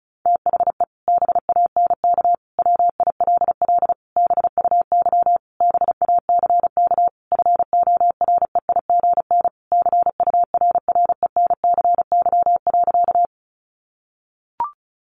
Morse code, heard as a keyed tone, about 34 wpm, 700 Hz. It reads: THE BANK WILL BUY BACK FOREIGN CURRENCY.
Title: Sentences from Top 1500 Words 35wpm